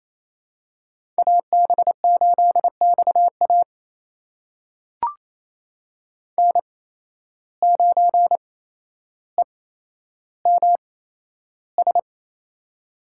Morse code, heard as a keyed tone, about 27 wpm, spaced as for 12 wpm, 700 Hz. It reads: AB8XA N9EMS
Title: USA Call Signs - 8x Character Spacing - 28wpm